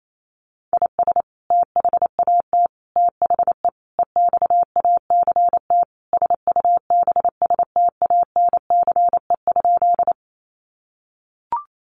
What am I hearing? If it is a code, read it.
Morse code, heard as a keyed tone, about 28 wpm, 700 Hz. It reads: IS THAT THE EXACT SUBSTANCE?
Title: Sentences from Top 1500 Words 28wpm